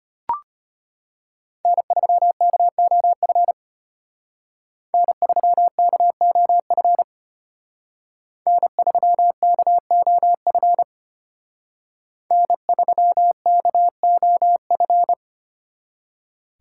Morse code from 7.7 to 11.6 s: N3KOF

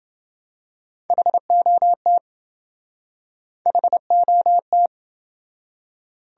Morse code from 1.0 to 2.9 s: HOT

